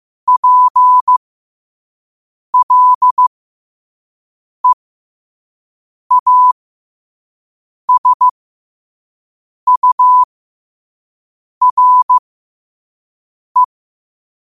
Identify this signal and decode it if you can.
Morse code, heard as a keyed tone, about 15 wpm, spaced as for 5 wpm, 1000 Hz. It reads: PLEASURE
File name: qotdfiles / 5wpm